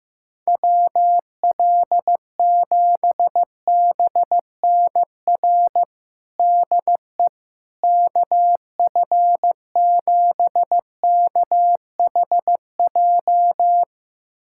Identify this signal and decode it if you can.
Morse code, heard as a keyed tone, about 15 wpm, 700 Hz. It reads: WL7BNR DE KF7KHJ
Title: USA Callsign DE Callsign - 6 Characters - 15wpm